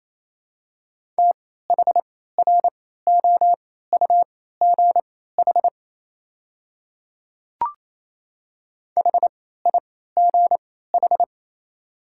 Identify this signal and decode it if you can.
Morse code, heard as a keyed tone, about 28 wpm, spaced as for 16 wpm, 700 Hz. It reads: THROUGH HIGH
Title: 3x Character Spacing - Top 200 Words - 28wpm